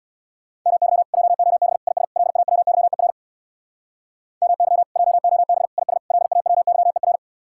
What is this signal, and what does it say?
Morse code, heard as a keyed tone, about 66 wpm, 700 Hz. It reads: MY JOB IS LAWYER MY JOB IS LAWYER